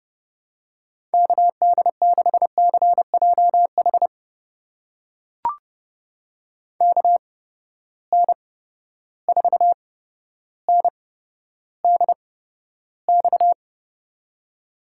Morse code, heard as a keyed tone, about 31 wpm, spaced as for 13 wpm, 700 Hz. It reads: KD6CJH KN4NDX
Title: USA Call Signs - 8x Character Spacing - 30wpm